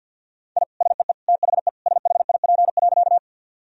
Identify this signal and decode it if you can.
Morse code, heard as a keyed tone, about 50 wpm, 700 Hz. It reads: I SEE THE SHIP.
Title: Sentences from Top 1000 Words - Review Part 2 - 50wpm